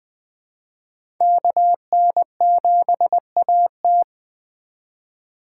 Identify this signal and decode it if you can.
Morse code, heard as a keyed tone, about 21 wpm, 700 Hz. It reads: KN7AT
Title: USA Call Signs - 5 Characters - 3x Character Spacing - 20wpm